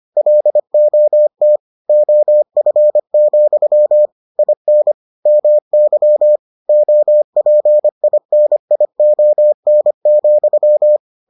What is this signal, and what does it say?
Morse code, heard as a keyed tone, about 25 wpm, 600 Hz. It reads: LOT OF, IN MY OPINION,